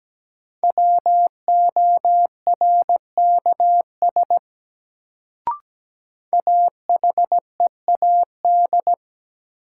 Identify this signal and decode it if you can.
Morse code, heard as a keyed tone, about 17 wpm, 700 Hz. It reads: WORKS AHEAD
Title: Sentences from Top 1100 Words - New Word Drill Alt - 17wpm